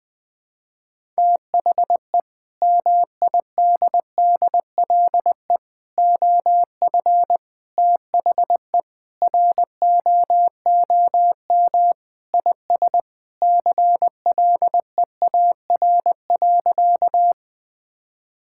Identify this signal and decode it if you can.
Morse code, heard as a keyed tone, about 21 wpm, 700 Hz. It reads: THE MIDDLE OF THE ROOM IS CLEAR.
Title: Sentences from Top 1000 Words - Review Part 6 - 20wpm